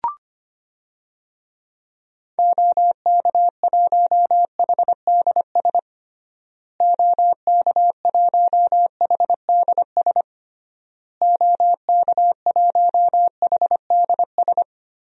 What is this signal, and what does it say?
Morse code, heard as a keyed tone, about 25 wpm, 700 Hz. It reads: OK1HDS OK1HDS OK1HDS
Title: DXCC-100 Call Signs - 6 Characters - Repeated 3x - 25wpm